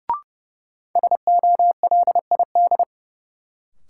Morse code, heard as a keyed tone, about 30 wpm, 700 Hz. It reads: SOLID
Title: Sentences from Top 1500 Words - New Word Drill Alt - 30wpm